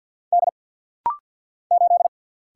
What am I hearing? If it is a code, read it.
Morse code, heard as a keyed tone, about 50 wpm, 700 Hz. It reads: D 8
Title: Single Letter-Number Alt - Rapid-Fire 50wpm